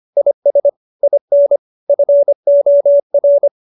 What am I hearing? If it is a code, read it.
Morse code, heard as a keyed tone, about 25 wpm, 570 Hz. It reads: IS IN FOR